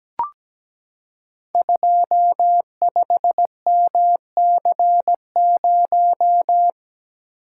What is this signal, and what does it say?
Morse code, heard as a keyed tone, about 17 wpm, 700 Hz. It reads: 25MC0
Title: Code Groups - 5 Letter:Numbers - 17wpm